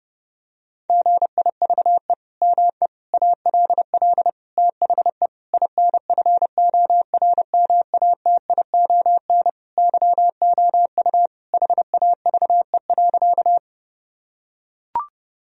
Morse code, heard as a keyed tone, about 32 wpm, 700 Hz. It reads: GIVE ME ALL THE INFORMATION YOU HAVE.
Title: Sentences from Top 1000 Words - Review Part 3 - 30wpm